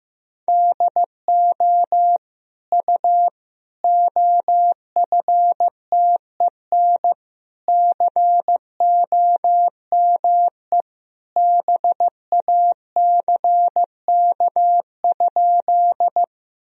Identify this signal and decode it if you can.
Morse code, heard as a keyed tone, about 14 wpm, 700 Hz. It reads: DO U OFTEN COME BACK?
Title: Sentences from Top 300 Words and Abbreviations 15wpm